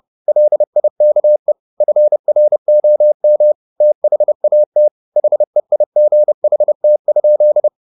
Morse code, heard as a keyed tone, about 30 wpm, 600 Hz. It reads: LIKE FROM THAT HEIGHT?
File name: the old man and the sea 2 30wpm